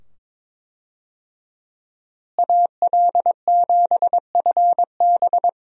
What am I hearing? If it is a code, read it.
Morse code, heard as a keyed tone, about 22 wpm, 700 Hz. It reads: AL7FB
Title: USA Call Signs - 5 Characters - Repeated 2x - 22wpm